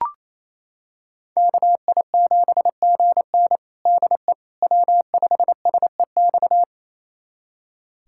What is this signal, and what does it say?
Morse code, heard as a keyed tone, about 28 wpm, 700 Hz. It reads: KI7GN DE W5SEX
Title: USA Callsign DE Callsign - 5 Characters - 28wpm